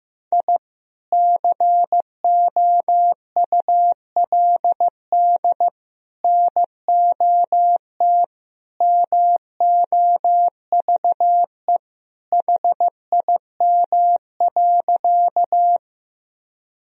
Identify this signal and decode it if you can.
Morse code, heard as a keyed tone, about 15 wpm, 700 Hz. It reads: I COULD NOT MOVE HIM.
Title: Sentences from Top 1000 Words - Review Part 1 - 15wpm